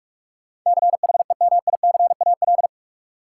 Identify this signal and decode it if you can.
Morse code, heard as a keyed tone, about 42 wpm, 700 Hz. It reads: CHEMICAL
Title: Sentences from Top 1200 Words - New Word Drill - 45wpm